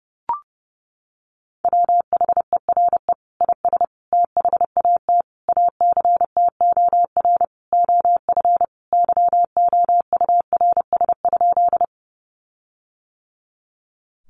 Morse code, heard as a keyed tone, about 30 wpm, 700 Hz. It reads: WHERE IS THAT ACTOR OF YOURS?
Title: Sentences from Top 1500 Words 30wpm